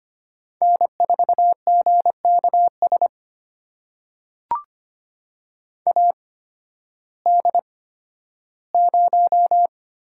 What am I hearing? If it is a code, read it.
Morse code, heard as a keyed tone, about 25 wpm, spaced as for 13 wpm, 700 Hz. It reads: N4GKS AD0
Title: USA Call Signs - 5 Characters - 8x Character Spacing - 25wpm